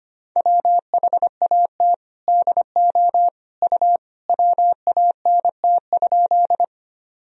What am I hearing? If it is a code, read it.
Morse code, heard as a keyed tone, about 25 wpm, 700 Hz. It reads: WHAT DO U WANT?
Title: Sentences from Top 200 Words and Abbreviations 25wpm